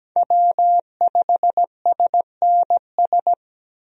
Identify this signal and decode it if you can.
Morse code, heard as a keyed tone, about 17 wpm, 700 Hz. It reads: W5SNS
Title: USA Call Signs - 5 Characters - 17wpm